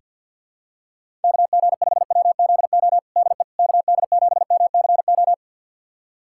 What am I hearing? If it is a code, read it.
Morse code, heard as a keyed tone, about 49 wpm, 700 Hz. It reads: KG5WZO DE KD7MCO
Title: USA Callsign DE Callsign 50wpm